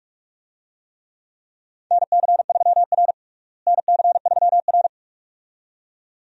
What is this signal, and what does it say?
Morse code, heard as a keyed tone, about 41 wpm, 700 Hz. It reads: NC3R NC3R
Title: USA Call Signs - 4 Characters - Repeated 2x - 45wpm